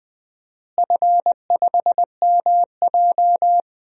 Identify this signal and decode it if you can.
Morse code, heard as a keyed tone, about 20 wpm, 700 Hz. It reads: F5MJ
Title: DXCC-100 Call Signs - 4 Characters - Repeated 2x - 20wpm